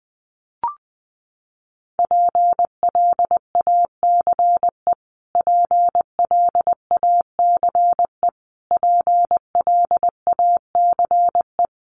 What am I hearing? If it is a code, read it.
Morse code, heard as a keyed tone, about 20 wpm, 700 Hz. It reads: PLACE PLACE PLACE
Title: Top 300 Words Repeated 20wpm